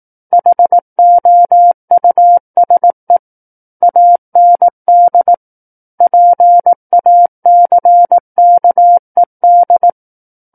Morse code, heard as a keyed tone, about 18 wpm, 700 Hz. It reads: HOUSE AND PACKED
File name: 18 WPM